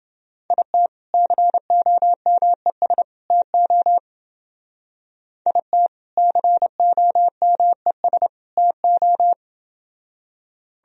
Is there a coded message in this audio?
Yes: Morse code from 0.5 to 9.3 s, heard as a keyed tone, about 28 wpm, 700 Hz. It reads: IT COMES TO IT COMES TO